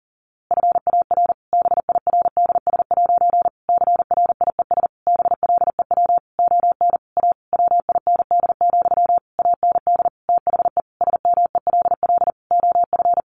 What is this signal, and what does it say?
Morse code, heard as a keyed tone, about 40 wpm, 700 Hz. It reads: FAR BIRDS' CRIES BLEW ON A WIND, AND THE SMELL OF
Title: A Sound Of Thunder 40wpm